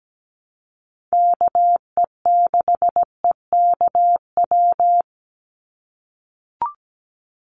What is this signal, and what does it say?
Morse code, heard as a keyed tone, about 17 wpm, 700 Hz. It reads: KE6EKW
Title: USA Call Signs 17wpm